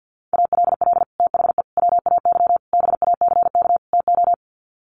Morse code, heard as a keyed tone, about 50 wpm, 700 Hz. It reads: ALL THE WAY BACK TO
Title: Sets of 5 Words Encore 50wpm